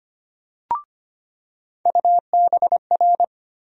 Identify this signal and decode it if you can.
Morse code, heard as a keyed tone, about 25 wpm, 700 Hz. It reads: UBR
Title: USA Call Sign Postfixes - 3 Letters 25wpm